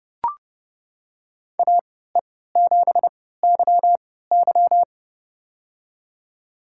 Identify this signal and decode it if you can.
Morse code, heard as a keyed tone, about 30 wpm, spaced as for 17 wpm, 700 Hz. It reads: AE7YY
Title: USA Call Signs - 5 Characters - 3x Character Spacing - 30wpm